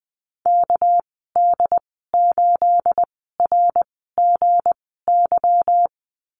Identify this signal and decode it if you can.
Morse code, heard as a keyed tone, about 20 wpm, spaced as for 14 wpm, 700 Hz. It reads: KD8RGY
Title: 231 Lesson 37g - Letter x - Call Signs 20wpm